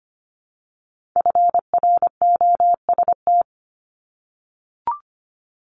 Morse code, heard as a keyed tone, about 25 wpm, 700 Hz. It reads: FROST
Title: Multi-letter Phonemes Repeated 3x 25wpm